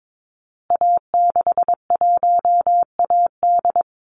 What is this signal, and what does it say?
Morse code, heard as a keyed tone, about 22 wpm, 700 Hz. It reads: A61AD
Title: Call Signs Repeated 3x - 22wpm